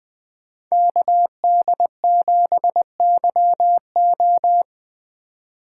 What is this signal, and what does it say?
Morse code, heard as a keyed tone, about 20 wpm, 700 Hz. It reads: KD7YO
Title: Speed Racing - USA 2x2 Call Signs 17wpm